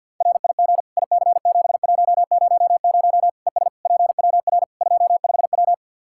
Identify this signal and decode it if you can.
Morse code, heard as a keyed tone, about 49 wpm, 700 Hz. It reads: RIG IC7100 ES PWR 25W
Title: K1USN SST QSO Practice 50wpm